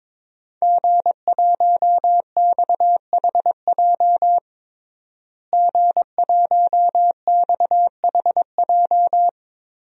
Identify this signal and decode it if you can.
Morse code, heard as a keyed tone, about 22 wpm, 700 Hz. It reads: G1XHJ G1XHJ
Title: DXCC-100 Call Signs - 5 Characters - Repeated 3x - 22wpm